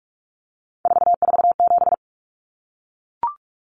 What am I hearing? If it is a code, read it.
Morse code, heard as a keyed tone, about 44 wpm, 700 Hz. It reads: 447
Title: Common Cut Numbers - 3-Digit 45wpm